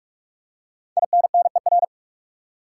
Morse code, heard as a keyed tone, about 47 wpm, 700 Hz. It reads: INNER
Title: Five Letter Words - 45wpm